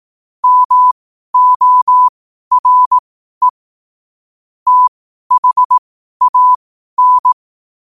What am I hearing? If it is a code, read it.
Morse code, heard as a keyed tone, about 18 wpm, spaced as for 12 wpm, 1000 Hz. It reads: MORE THAN
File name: qotdfiles / 12wpm